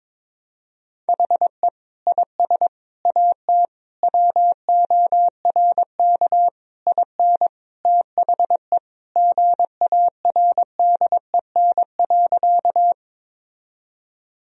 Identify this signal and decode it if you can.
Morse code, heard as a keyed tone, about 22 wpm, 700 Hz. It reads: HE IS AT WORK IN THE GARDEN.
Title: Sentences from Top 1000 Words - Review Part 4 - 22wpm